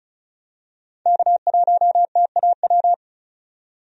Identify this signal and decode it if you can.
Morse code, heard as a keyed tone, about 36 wpm, 700 Hz. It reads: K1TAW
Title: USA Call Signs - 5 Characters - 35wpm